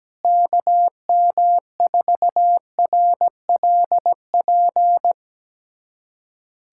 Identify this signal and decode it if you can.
Morse code, heard as a keyed tone, about 17 wpm, 700 Hz. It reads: KM4RLP
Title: USA Callsign DE Callsign - 6 Characters - 17wpm